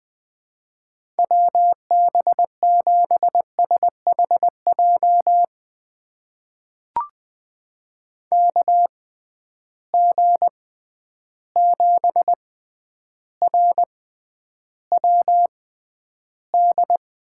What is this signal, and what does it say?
Morse code, heard as a keyed tone, about 21 wpm, spaced as for 10 wpm, 700 Hz. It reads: WB7SHJ KG7RWD
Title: USA Call Signs - 6 Characters - 6x Character Spacing - 20wpm